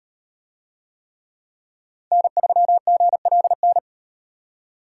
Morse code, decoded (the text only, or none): N3GLN